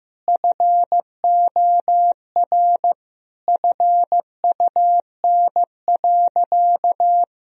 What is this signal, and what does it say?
Morse code, heard as a keyed tone, about 15 wpm, 700 Hz. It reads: FOR FUN.
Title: Sentences from Top 1000 Words - Review Part 3 - 15wpm